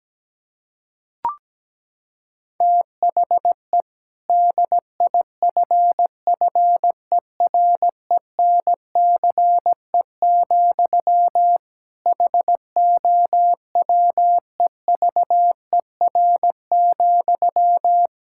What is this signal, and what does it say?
Morse code, heard as a keyed tone, about 17 wpm, 700 Hz. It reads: THE DIFFERENCE, HOWEVER,
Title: Sentences from Top 1000 Words - Review Part 5 - 17wpm